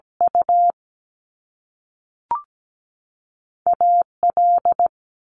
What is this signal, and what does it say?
Morse code, heard as a keyed tone, about 17 wpm, spaced as for 3 wpm, 700 Hz. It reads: UA L
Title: Top Two Letter Combinations in Words 17wpm